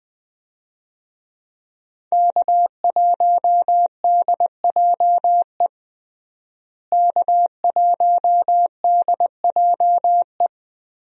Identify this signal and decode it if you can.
Morse code, heard as a keyed tone, about 21 wpm, 700 Hz. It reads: K1DJE K1DJE
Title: USA Call Signs - 5 Characters - Repeated 3x - 20wpm